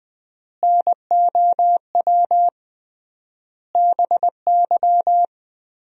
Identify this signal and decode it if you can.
Morse code, heard as a keyed tone, about 20 wpm, 700 Hz. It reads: NOW BY
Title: 3x Word-Spacing - Sentences from Top 300 Words - 20wpm